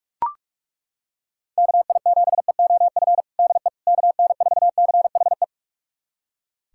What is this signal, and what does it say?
Morse code, heard as a keyed tone, about 43 wpm, 700 Hz. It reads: KI7EOF DE KN4CHE
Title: USA Callsign DE Callsign - 6 Characters - 45wpm